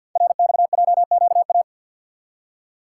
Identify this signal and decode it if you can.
Morse code, heard as a keyed tone, about 46 wpm, 700 Hz. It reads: RXJQA
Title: Code Groups - 5 Letters - 50wpm